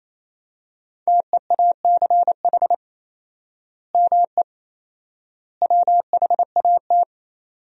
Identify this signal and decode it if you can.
Morse code, heard as a keyed tone, about 28 wpm, 700 Hz. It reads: TEACH ME WHAT
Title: 4x Word-Spacing - Sentences from Top 500 Words - 28wpm